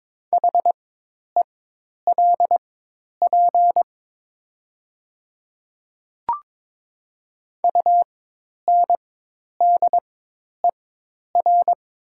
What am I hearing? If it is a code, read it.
Morse code, heard as a keyed tone, about 22 wpm, spaced as for 10 wpm, 700 Hz. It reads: HELP UNDER
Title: 4x Character Spacing - Top 300 Words - 22wpm